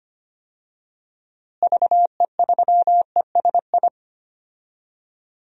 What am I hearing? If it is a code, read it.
Morse code, heard as a keyed tone, about 25 wpm, 700 Hz. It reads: VE3ESI
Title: DXCC-100 Call Signs - 6 Characters - Repeated 2x - 25wpm